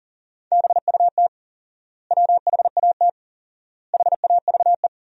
Morse code, heard as a keyed tone, about 37 wpm, 700 Hz. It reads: BUT WHAT HAVE
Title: 4x Word-Spacing - Sentences from Top 100 Words - 40wpm